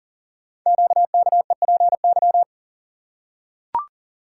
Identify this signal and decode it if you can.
Morse code, heard as a keyed tone, about 42 wpm, 700 Hz. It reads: QKEPY
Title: Code Groups - 5 Letters - 40wpm